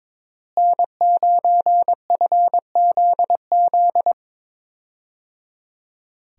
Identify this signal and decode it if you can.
Morse code, heard as a keyed tone, about 22 wpm, 700 Hz. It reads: N9FZZ
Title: USA Call Signs - 5 Characters - Repeated 3x - 22wpm